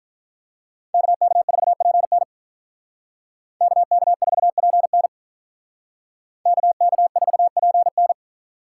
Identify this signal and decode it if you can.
Morse code, heard as a keyed tone, about 46 wpm, 700 Hz. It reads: KK4PN KK4PN KK4PN